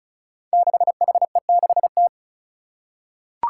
Morse code, heard as a keyed tone, about 36 wpm, 700 Hz. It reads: 6HE6T
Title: Five Challenging Letter-Number Combinations 35wpm